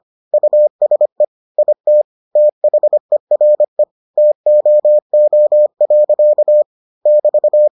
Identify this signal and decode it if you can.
Morse code, heard as a keyed tone, about 25 wpm, 600 Hz. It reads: USE IT THERE TOO. <BT>